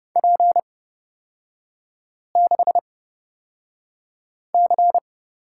Morse code, heard as a keyed tone, about 30 wpm, spaced as for 5 wpm, 700 Hz. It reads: P6C
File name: Lesson 36e - BT - Three Characters 30wpm